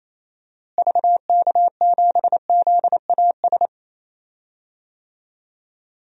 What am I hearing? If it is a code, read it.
Morse code, heard as a keyed tone, about 28 wpm, 700 Hz. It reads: VK7ZAS
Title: DXCC-100 Call Signs - 6 Characters - Repeated 2x - 28wpm